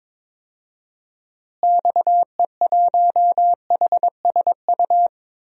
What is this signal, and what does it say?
Morse code, heard as a keyed tone, about 22 wpm, 700 Hz. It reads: XE1HSU
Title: DXCC-100 Call Signs - 6 Characters - Repeated 3x - 22wpm